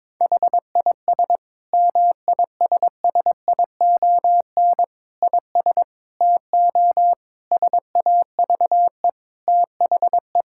Morse code, heard as a keyed tone, about 22 wpm, 700 Hz. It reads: HIS MISSION IS TO SAVE THE
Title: Sentences from Top 1600 Words 22wpm